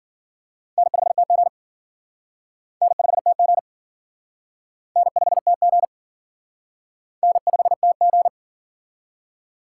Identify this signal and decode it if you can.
Morse code, heard as a keyed tone, about 52 wpm, 700 Hz. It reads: N5TG N5TG N5TG N5TG